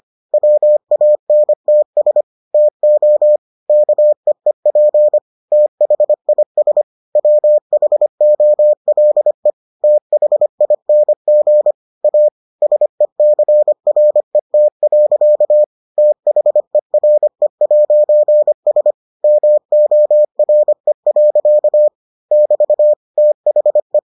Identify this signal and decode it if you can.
Morse code, heard as a keyed tone, about 25 wpm, 600 Hz. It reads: WANTS TO KEEP THIS WHOLE THING A SECRET. THERE'S MORE. <BT> THE